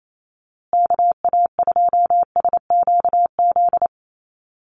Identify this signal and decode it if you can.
Morse code, heard as a keyed tone, about 28 wpm, 700 Hz. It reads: KA2SQZ
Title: USA Call Signs Repeated 2x 28wpm